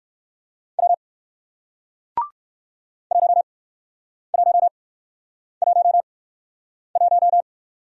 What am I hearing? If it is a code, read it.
Morse code, heard as a keyed tone, about 63 wpm, spaced as for 9 wpm, 700 Hz. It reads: W 1111